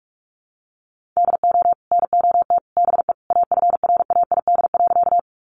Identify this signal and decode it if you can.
Morse code, heard as a keyed tone, about 44 wpm, 700 Hz. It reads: DO NOT BE AFRAID.
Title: Sentences from Top 1500 Words 45wpm